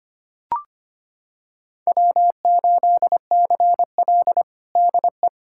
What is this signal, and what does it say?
Morse code, heard as a keyed tone, about 25 wpm, 700 Hz. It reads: W8CL DE
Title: USA Callsign DE Callsign - 4 Characters - 25wpm